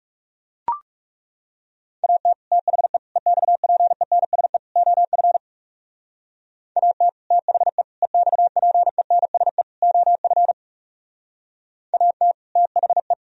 Morse code, heard as a keyed tone, about 40 wpm, 700 Hz. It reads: AT THE EXPENSE OF AT THE EXPENSE OF AT THE